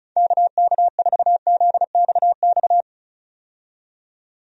Morse code, heard as a keyed tone, about 36 wpm, 700 Hz. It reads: KK4ZXX
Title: USA Call Signs - 6 Characters - 35wpm